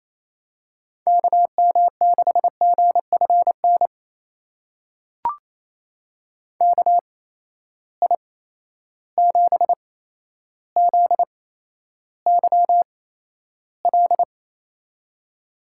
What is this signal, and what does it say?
Morse code, heard as a keyed tone, about 28 wpm, spaced as for 12 wpm, 700 Hz. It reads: KM6GFN KI7ZYL
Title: USA Call Signs - 6 Characters - 8x Character Spacing - 28wpm